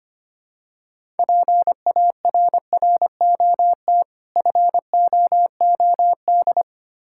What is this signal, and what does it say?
Morse code, heard as a keyed tone, about 25 wpm, 700 Hz. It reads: PARROT FOOD
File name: practice / 25wpm / Difficult Words with Hints - 25wpm